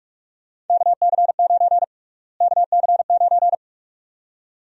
Morse code, heard as a keyed tone, about 46 wpm, 700 Hz. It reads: KC9 KC9
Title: USA Call Sign Prefixes - Repeated 2x - 45wpm